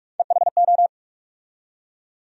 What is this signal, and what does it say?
Morse code, heard as a keyed tone, about 42 wpm, 700 Hz. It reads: EHO
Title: USA Call Sign Postfixes - 3 Letters 45wpm